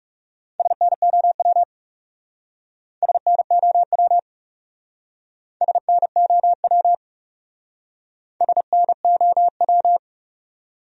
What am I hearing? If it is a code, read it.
Morse code, heard as a keyed tone, about 40 wpm, 700 Hz. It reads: SNOW SNOW SNOW SNOW